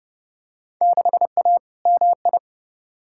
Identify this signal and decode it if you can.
Morse code, heard as a keyed tone, about 32 wpm, 700 Hz. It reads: 6A MI
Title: Field Day Exchanges - 30wpm